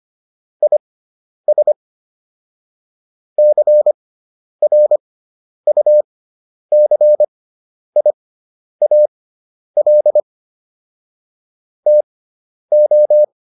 Morse code, heard as a keyed tone, about 25 wpm, spaced as for 10 wpm, 600 Hz. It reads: IS CRUCIAL TO